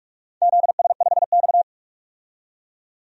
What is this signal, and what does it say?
Morse code, heard as a keyed tone, about 41 wpm, 700 Hz. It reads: ZS5X
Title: DXCC-100 Call Signs - 4 Characters - 45wpm